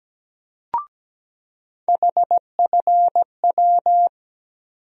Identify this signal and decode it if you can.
Morse code, heard as a keyed tone, about 17 wpm, 700 Hz. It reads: HFW
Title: USA Call Sign Postfixes - 3 Letters 17wpm